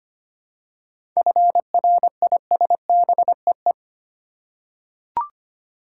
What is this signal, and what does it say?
Morse code, heard as a keyed tone, about 25 wpm, 700 Hz. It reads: FRISBEE
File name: practice / 25wpm / Difficult Words with Hints - 25wpm